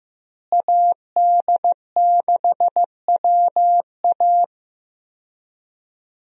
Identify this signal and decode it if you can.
Morse code, heard as a keyed tone, about 15 wpm, 700 Hz. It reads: AD6WA
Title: USA Call Signs - 5 Characters - Repeated 3x - 15wpm